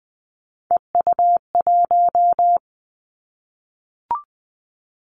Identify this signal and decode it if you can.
Morse code, heard as a keyed tone, about 20 wpm, 700 Hz. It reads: EU1
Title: Callsign Prefixes for Top 50 DXCC - 20wpm